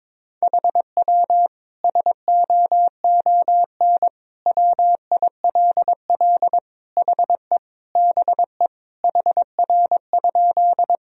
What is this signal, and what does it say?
Morse code, heard as a keyed tone, about 22 wpm, 700 Hz. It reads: HW SOON WILL HE BE HR?
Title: Sentences from Top 400 Words and Abbreviations 22wpm